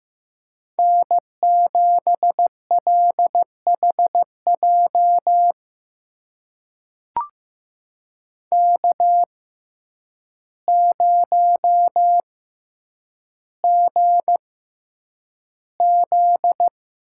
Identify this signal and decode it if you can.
Morse code, heard as a keyed tone, about 15 wpm, spaced as for 8 wpm, 700 Hz. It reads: N7LHJ K0GZ
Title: USA Call Signs - 5 Characters - 6x Character Spacing - 15wpm